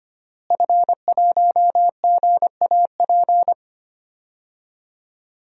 Morse code, heard as a keyed tone, about 25 wpm, 700 Hz. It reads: F1GAP
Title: DXCC-100 Call Signs - 5 Characters - Repeated 3x - 25wpm